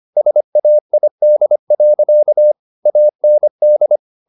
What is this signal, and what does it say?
Morse code, heard as a keyed tone, about 25 wpm, 600 Hz. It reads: SAID. AND
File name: the old man and the sea part 1 25wpm